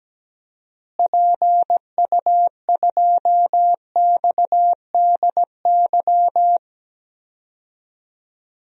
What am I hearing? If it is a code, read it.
Morse code, heard as a keyed tone, about 17 wpm, 700 Hz. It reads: PU2XDY
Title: DXCC-100 Call Signs - 6 Characters - Repeated 2x - 17wpm